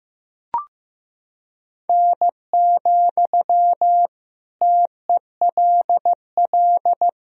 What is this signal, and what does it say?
Morse code, heard as a keyed tone, about 15 wpm, 700 Hz. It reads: N, TELL
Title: Sentences from Top 200 Words and Abbreviations 15wpm